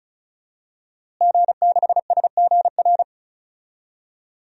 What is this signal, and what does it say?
Morse code, heard as a keyed tone, about 36 wpm, 700 Hz. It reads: G6SGR
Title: DXCC-100 Call Signs - 5 Characters - Repeated 3x - 35wpm